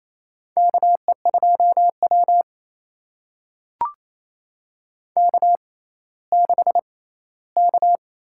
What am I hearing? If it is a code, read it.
Morse code, heard as a keyed tone, about 28 wpm, spaced as for 16 wpm, 700 Hz. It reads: KE2W K6K
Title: USA Call Signs - 4 Characters - 6x Character Spacing - 28wpm